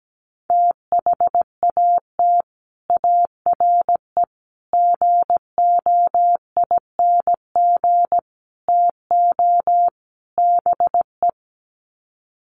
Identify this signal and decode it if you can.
Morse code, heard as a keyed tone, about 17 wpm, 700 Hz. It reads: THAT ARE GOING TO BE
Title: Sets of 5 Words Encore2 17wpm